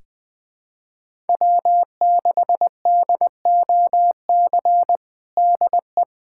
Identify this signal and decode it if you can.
Morse code, heard as a keyed tone, about 21 wpm, 700 Hz. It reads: W6DOC DE
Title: USA Callsign DE Callsign - 5 Characters - 20wpm